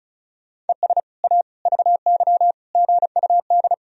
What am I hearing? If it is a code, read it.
Morse code, heard as a keyed tone, about 34 wpm, 700 Hz. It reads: ES A VY GUD
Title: Sentences from Top 200 Words and Abbreviations 35wpm